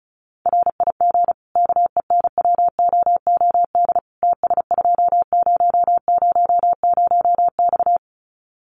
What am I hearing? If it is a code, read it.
Morse code, heard as a keyed tone, about 34 wpm, 700 Hz. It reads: RIG KENWOOD TS2000X
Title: Common QSO Phrases 35wpm